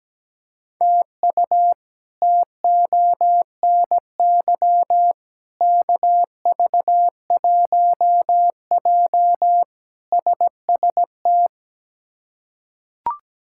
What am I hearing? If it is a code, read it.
Morse code, heard as a keyed tone, about 17 wpm, 700 Hz. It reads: TU TONY KV1J SST
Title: K1USN SST QSO Practice - 17wpm